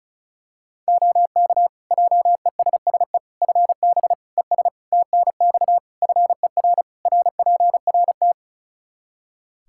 Morse code, heard as a keyed tone, about 36 wpm, 700 Hz. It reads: OK JESSE FB ES TNX FER RPRT
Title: Practice QSO - Exchange 2 - Info - 35wpm